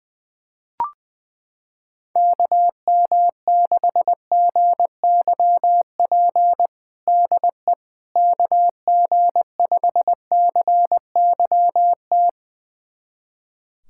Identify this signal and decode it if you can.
Morse code, heard as a keyed tone, about 20 wpm, 700 Hz. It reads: KM6GYP DE KG5CYT
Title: USA Callsign DE Callsign - 6 Characters - 20wpm